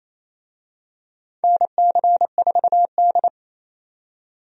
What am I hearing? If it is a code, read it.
Morse code, heard as a keyed tone, about 27 wpm, 700 Hz. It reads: NC4D
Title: USA Call Signs - 4 Characters - 28wpm